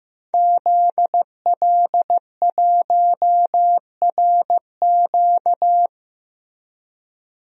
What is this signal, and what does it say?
Morse code, heard as a keyed tone, about 15 wpm, 700 Hz. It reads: ZL1RQ
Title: DXCC-100 Call Signs - 5 Characters - Repeated 2x - 15wpm